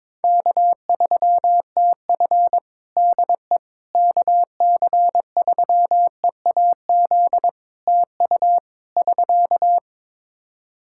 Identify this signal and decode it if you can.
Morse code, heard as a keyed tone, about 22 wpm, 700 Hz. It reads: K3TF DE KC3EAZ TU <SK>
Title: Practice QSO - Exchange 4 - Ending - 22wpm